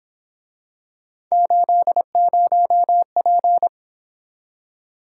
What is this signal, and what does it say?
Morse code, heard as a keyed tone, about 27 wpm, 700 Hz. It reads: 80P
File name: Speed-Racing - Three Letter-Number Combinations - 22wpm